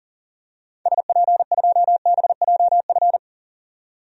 Morse code, heard as a keyed tone, about 40 wpm, 700 Hz. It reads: SP2BJF
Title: DXCC-100 Call Signs - 6 Characters - 40wpm